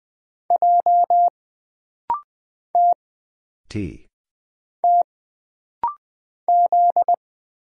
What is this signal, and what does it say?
Morse code, heard as a keyed tone, about 20 wpm, 700 Hz. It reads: J T T Z
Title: Single Letter-Number - Rapid-Fire 20wpm